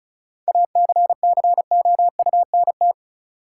Morse code, heard as a keyed tone, about 34 wpm, 700 Hz. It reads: ACCOUNT
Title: Sentences from Top 900 Words - New Word Drill - 35wpm